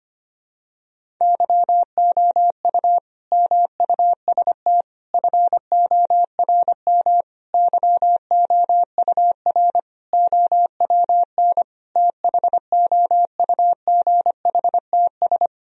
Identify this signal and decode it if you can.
Morse code, heard as a keyed tone, about 25 wpm, 700 Hz. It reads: YOU MUST FORM YOUR OWN THOUGHTS
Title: Sentences from Top 1000 Words - Review Part 1 - 25wpm